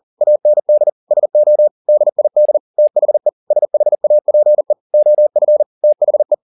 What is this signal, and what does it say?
Morse code, heard as a keyed tone, about 40 wpm, 600 Hz. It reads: AND SO DID THE SHAPE OF THE